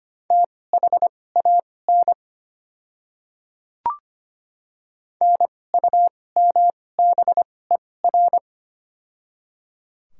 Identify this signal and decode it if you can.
Morse code, heard as a keyed tone, about 25 wpm, spaced as for 18 wpm, 690 Hz. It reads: THAN NUMBER
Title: 2x Character Spacing - Top 100 Words - 25wpm